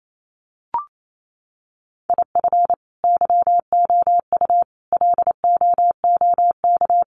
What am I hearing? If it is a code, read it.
Morse code, heard as a keyed tone, about 28 wpm, 700 Hz. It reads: IF YOU LOOK
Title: Sets of 4 Words 28wpm